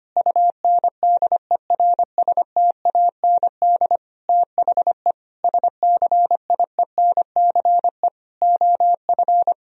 Morse code, heard as a keyed tone, about 25 wpm, 700 Hz. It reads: UNDERSTAND THE SCIENCE OF
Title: Sentences from Top 1000 Words - Review Part 2 - 25wpm